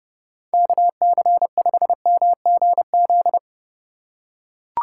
Morse code, heard as a keyed tone, about 31 wpm, 700 Hz. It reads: KC5MGZ